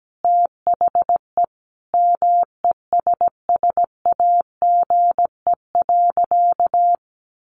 Morse code, heard as a keyed tone, about 17 wpm, 700 Hz. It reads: THE MESSAGE.
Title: Sentences from Top 1100 Words 17wpm